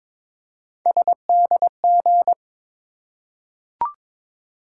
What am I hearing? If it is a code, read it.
Morse code, heard as a keyed tone, about 22 wpm, 700 Hz. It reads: SDG
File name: ARRL Sections v2 - 22wpm